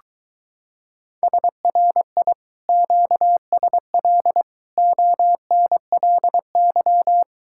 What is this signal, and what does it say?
Morse code, heard as a keyed tone, about 23 wpm, 700 Hz. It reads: SRI QSL ONLY